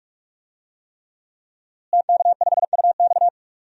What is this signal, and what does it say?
Morse code, heard as a keyed tone, about 46 wpm, 700 Hz. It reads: TK5UX
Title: DXCC-100 Call Signs - 5 Characters - Repeated 3x - 45wpm